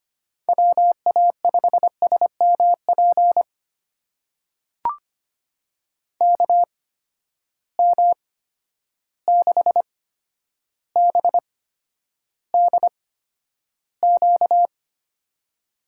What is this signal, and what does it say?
Morse code, heard as a keyed tone, about 25 wpm, spaced as for 11 wpm, 700 Hz. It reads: WA5SMP KM6BDQ
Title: USA Call Signs - 6 Characters - 8x Character Spacing - 25wpm